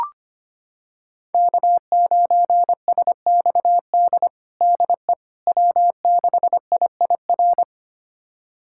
Morse code, heard as a keyed tone, about 25 wpm, 700 Hz. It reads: K9SXD DE W6IIR
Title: USA Callsign DE Callsign - 5 Characters - 25wpm